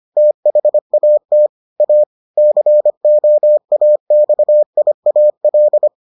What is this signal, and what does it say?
Morse code, heard as a keyed tone, about 25 wpm, 600 Hz. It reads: THAT A COAXIAL